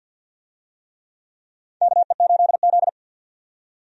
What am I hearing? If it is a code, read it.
Morse code, heard as a keyed tone, about 48 wpm, 700 Hz. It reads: KE8Z
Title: USA Call Signs - 4 Characters - 50wpm